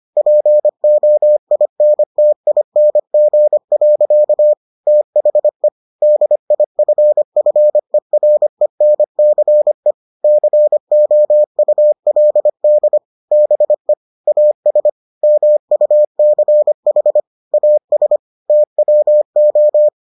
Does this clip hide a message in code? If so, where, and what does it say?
Yes: Morse code from 0.2 to 19.9 s, heard as a keyed tone, about 25 wpm, 600 Hz. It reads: POINTING. THE DIFFERENCE COULD BE AS MUCH AS TWO